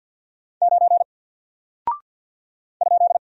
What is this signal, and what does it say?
Morse code, heard as a keyed tone, about 49 wpm, 700 Hz. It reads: 9 ?
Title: Single Character (Letter:Number:Punctuation:Prosign) Alt 50wpm